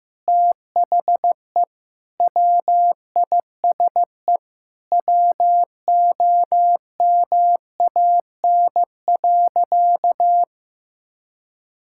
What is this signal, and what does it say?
Morse code, heard as a keyed tone, about 15 wpm, 700 Hz. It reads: THE WISE WOMAN.
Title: Sentences from Top 1600 Words 15wpm